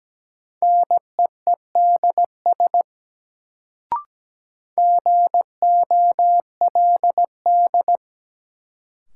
Morse code, heard as a keyed tone, about 17 wpm, 700 Hz. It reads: NEEDS GOLD
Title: Sentences from Top 1000 Words - Review Part 3 - New Word Drill - 17wpm